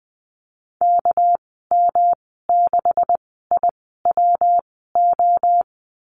Morse code, heard as a keyed tone, about 20 wpm, spaced as for 15 wpm, 700 Hz. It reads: KM6IWO
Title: 231 Lesson 37g - Letter x - Call Signs 20wpm